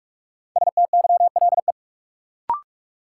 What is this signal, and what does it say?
Morse code, heard as a keyed tone, about 43 wpm, 700 Hz. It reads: STYLE
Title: Sentences from Top 800 Words - New Word Drill - 45wpm